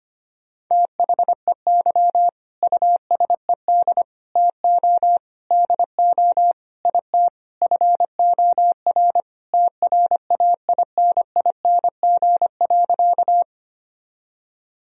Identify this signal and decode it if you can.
Morse code, heard as a keyed tone, about 25 wpm, 700 Hz. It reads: THEY USED TO DO IT FOR TRAINING.
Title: Sentences from Top 1000 Words - Review Part 4 - 25wpm